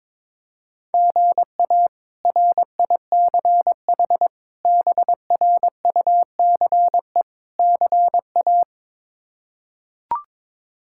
Morse code, heard as a keyed tone, about 22 wpm, 700 Hz. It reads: GA RICH BRUCE CA
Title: K1USN SST QSO Practice - 22wpm